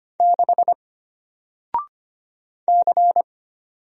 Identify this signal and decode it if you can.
Morse code, heard as a keyed tone, about 25 wpm, 700 Hz. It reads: 6 C
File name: Single-Letter-Number-25wpm